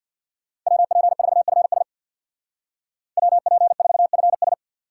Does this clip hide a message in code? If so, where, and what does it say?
Yes: Morse code from 0.7 to 4.5 s, heard as a keyed tone, about 63 wpm, 700 Hz. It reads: WP4FS WP4FS